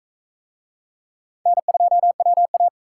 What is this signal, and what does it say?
Morse code, heard as a keyed tone, about 36 wpm, 700 Hz. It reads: N2WA